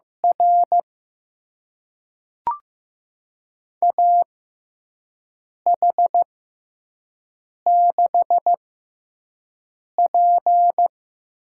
Morse code, heard as a keyed tone, about 15 wpm, spaced as for 5 wpm, 700 Hz. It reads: R AH6P
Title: USA Call Signs - 4 Characters - 6x Character Spacing - 15wpm